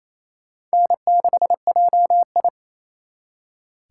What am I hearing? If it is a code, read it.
Morse code, heard as a keyed tone, about 35 wpm, 700 Hz. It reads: N6JI
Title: Speed Racing - USA 1x2 Call Signs 28wpm